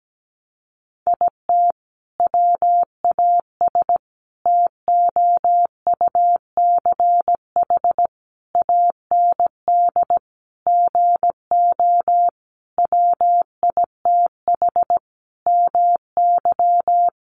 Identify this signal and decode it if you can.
Morse code, heard as a keyed tone, about 17 wpm, 700 Hz. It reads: IT WAS TOUCH AND GO WITH MY
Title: Sentences from Top 1200 Words 17wpm